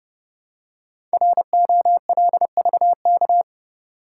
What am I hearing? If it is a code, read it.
Morse code, heard as a keyed tone, about 30 wpm, 700 Hz. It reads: ROLVK
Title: Code Groups - 5 Letter:Numbers - 30wpm